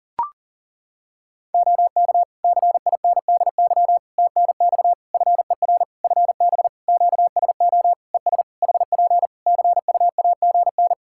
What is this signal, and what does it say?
Morse code, heard as a keyed tone, about 40 wpm, 700 Hz. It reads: OK CINDY TNX FER FB QSO ES HP CUAGN
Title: K1USN SST QSO Practice 40wpm